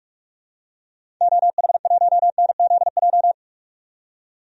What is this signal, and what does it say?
Morse code, heard as a keyed tone, about 46 wpm, 700 Hz. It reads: OH1NZJ
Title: DXCC-100 Call Signs - 6 Characters - 45wpm